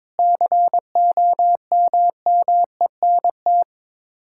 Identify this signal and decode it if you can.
Morse code, heard as a keyed tone, about 21 wpm, 700 Hz. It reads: COMMENT
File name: practice / 22wpm / Sets of 8 Words - 22wpm